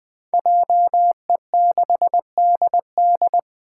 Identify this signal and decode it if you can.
Morse code, heard as a keyed tone, about 20 wpm, 700 Hz. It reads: JE6DD
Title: Five Challenging Letter-Number Combinations 20wpm